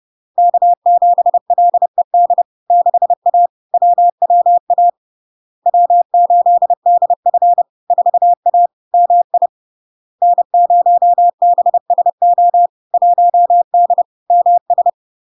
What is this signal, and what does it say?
Morse code, heard as a keyed tone, about 30 wpm, 700 Hz. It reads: K7LED 6A WWA W8DF 4A MI N0BSO 1D MS
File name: LARK-FD-2022-Log-in-CW-30wpm-700-Hz